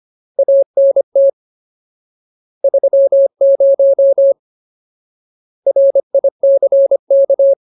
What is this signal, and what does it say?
Morse code, heard as a keyed tone, about 25 wpm, 550 Hz. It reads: ANT 30 RICK
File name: Ultimate Word List / 25WPM Ultimate